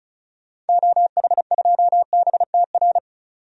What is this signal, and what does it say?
Morse code, heard as a keyed tone, about 34 wpm, 700 Hz. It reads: OH2BTR
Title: DXCC-100 Call Signs - 6 Characters - 35wpm